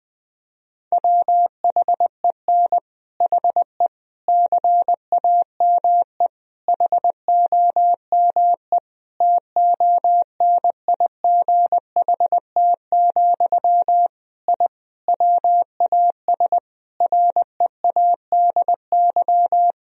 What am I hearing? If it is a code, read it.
Morse code, heard as a keyed tone, about 20 wpm, 700 Hz. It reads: WHEN HE CAME HOME TONIGHT, I WAS READY
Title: Sentences from Top 1000 Words - Review Part 3 - 20wpm